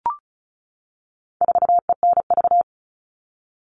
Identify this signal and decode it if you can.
Morse code, heard as a keyed tone, about 34 wpm, 700 Hz. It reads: 4ENV
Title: All Cut Numbers - 4-Digit 35wpm